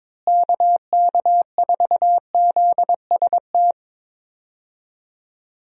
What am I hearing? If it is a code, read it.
Morse code, heard as a keyed tone, about 22 wpm, 700 Hz. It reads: KK4ZST
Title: USA Call Signs - 6 Characters - Repeated 3x - 22wpm